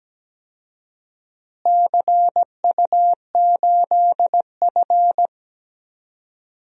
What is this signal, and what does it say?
Morse code, heard as a keyed tone, about 17 wpm, 700 Hz. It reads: CU8F
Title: DXCC-100 Call Signs - 4 Characters - Repeated 2x - 17wpm